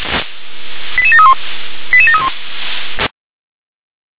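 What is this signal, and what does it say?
Five-tone selective call (ZVEI), heard as quick runs of five tones, tones of about 70 ms. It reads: 88731 88731
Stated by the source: Five-tone ZVEI selective call received on an SDR.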